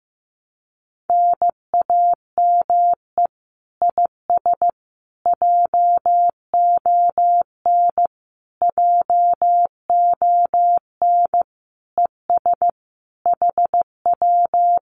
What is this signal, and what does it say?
Morse code, heard as a keyed tone, about 15 wpm, 700 Hz. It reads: NAME IS JON JON ES HW
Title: Common QSO Phrases Encore 15wpm